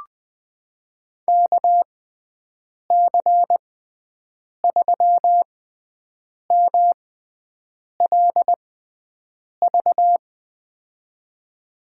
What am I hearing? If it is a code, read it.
Morse code, heard as a keyed tone, about 20 wpm, spaced as for 7 wpm, 700 Hz. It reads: KC3MLV
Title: USA Call Signs - 6 Characters - 6x Character Spacing - 20wpm